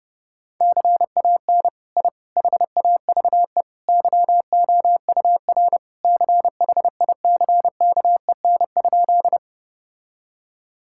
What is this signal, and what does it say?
Morse code, heard as a keyed tone, about 30 wpm, 700 Hz. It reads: CAN I HAVE YOUR CHICKEN?
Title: Sentences from Top 1000 Words - Review Part 5 - 30wpm